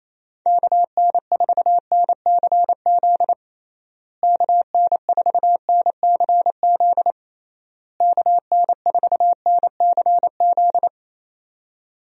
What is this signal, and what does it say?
Morse code, heard as a keyed tone, about 28 wpm, 700 Hz. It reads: KN4NCZ KN4NCZ KN4NCZ
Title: USA Call Signs - 6 Characters - Repeated 3x - 28wpm